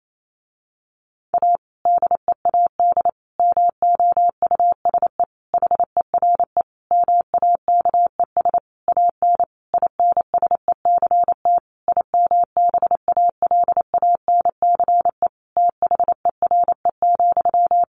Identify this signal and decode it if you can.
Morse code, heard as a keyed tone, about 28 wpm, 700 Hz. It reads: A DEAD MOUSE HERE MAKES AN INSECT IMBALANCE THERE,
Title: A Sound Of Thunder 28wpm